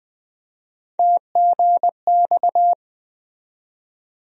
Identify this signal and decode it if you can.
Morse code, heard as a keyed tone, about 20 wpm, 700 Hz. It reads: TGX
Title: USA Call Sign Postfixes - 3 Letters 20wpm